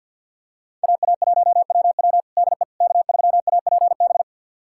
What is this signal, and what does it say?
Morse code, heard as a keyed tone, about 50 wpm, 690 Hz. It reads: AA1WW DE K3APB
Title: USA Callsign DE Callsign - 5 Characters - 50wpm